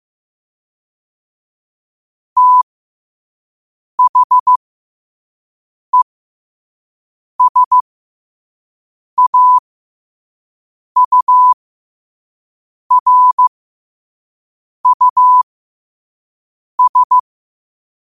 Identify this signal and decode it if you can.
Morse code, heard as a keyed tone, about 15 wpm, spaced as for 5 wpm, 1000 Hz. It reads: THESAURUS
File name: qotdfiles / 5wpm